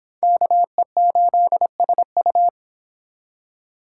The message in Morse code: KE8SU

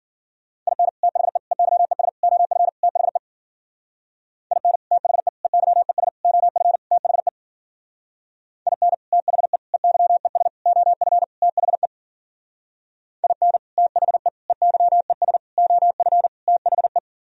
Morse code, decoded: IN THE EYES OF THE IN THE EYES OF THE IN THE EYES OF THE IN THE EYES OF THE

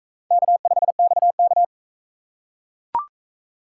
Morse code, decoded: K5XK

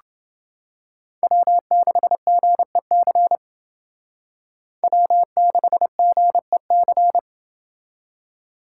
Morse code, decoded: W6GEC W6GEC

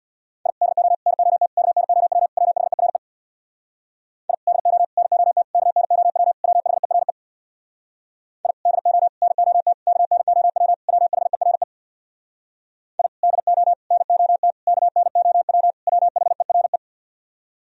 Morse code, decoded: I DO NOT KNOW WHERE I DO NOT KNOW WHERE I DO NOT KNOW WHERE I DO NOT KNOW WHERE

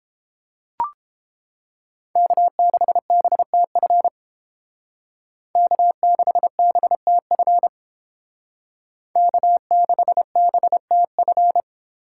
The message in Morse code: K6BTF K6BTF K6BTF

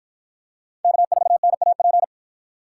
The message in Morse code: K4NAP

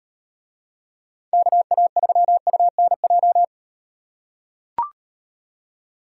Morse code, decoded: KA3UNJ

